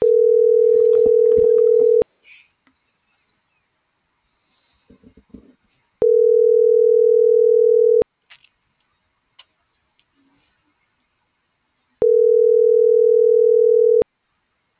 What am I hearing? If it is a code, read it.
Ringback tone (North America), 440 + 480 Hz.